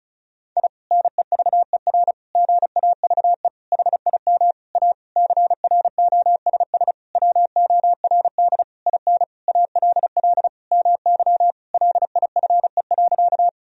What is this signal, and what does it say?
Morse code, heard as a keyed tone, about 36 wpm, 700 Hz. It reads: I NEVER GAVE HIM A CROSS WORD IN ALL MY LIFE.
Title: Sentences from Top 1000 Words - Review Part 1 - 35wpm